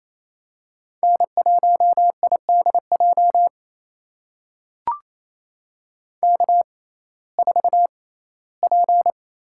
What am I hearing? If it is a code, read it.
Morse code, heard as a keyed tone, about 28 wpm, spaced as for 17 wpm, 700 Hz. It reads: N1IDJ K4P
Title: USA Call Signs - 6x Character Spacing - 28wpm